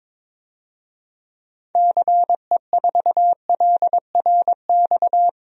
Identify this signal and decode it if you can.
Morse code, heard as a keyed tone, about 22 wpm, 700 Hz. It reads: CE4LRX
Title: DXCC-100 Call Signs - 6 Characters - 22wpm